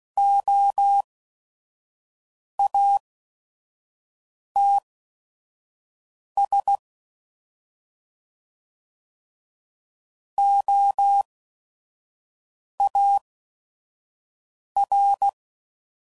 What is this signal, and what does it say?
Morse code, heard as a keyed tone, about 16 wpm, spaced as for 5 wpm, 800 Hz. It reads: OATS OAR
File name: Prefixes / By First Letter / Os